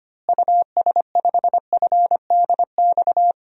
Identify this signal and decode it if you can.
Morse code, heard as a keyed tone, about 25 wpm, 700 Hz. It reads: US5FDX
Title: DXCC-100 Call Signs - 6 Characters - Repeated 2x - 25wpm